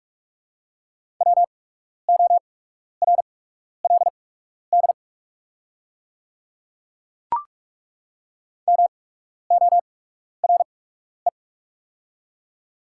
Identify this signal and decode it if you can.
Morse code, heard as a keyed tone, about 48 wpm, spaced as for 12 wpm, 690 Hz. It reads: WORLD MORE
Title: 8x Character Spacing - Top 500 Words - 45wpm